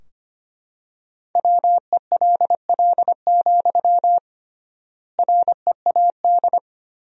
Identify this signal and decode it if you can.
Morse code, heard as a keyed tone, about 25 wpm, 700 Hz. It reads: WELL, READ
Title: 3x Word-Spacing - Sentences from Top 200 Words - 25wpm